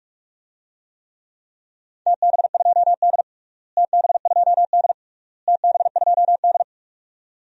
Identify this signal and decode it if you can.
Morse code, heard as a keyed tone, about 46 wpm, 700 Hz. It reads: TB2D TB2D TB2D
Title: DXCC-100 Call Signs - 4 Characters - Repeated 3x - 45wpm